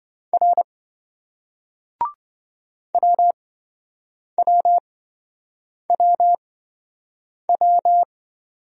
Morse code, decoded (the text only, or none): R WWWW